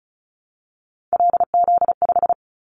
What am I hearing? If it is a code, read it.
Morse code, heard as a keyed tone, about 36 wpm, 700 Hz. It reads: LZ5
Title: Callsign Prefixes for Top 50 DXCC - 35wpm